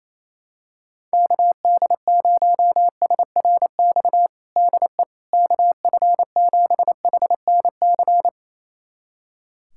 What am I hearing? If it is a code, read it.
Morse code, heard as a keyed tone, about 28 wpm, 700 Hz. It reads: KD0SRX DE KF7HNC
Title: USA Callsign DE Callsign - 6 Characters - 28wpm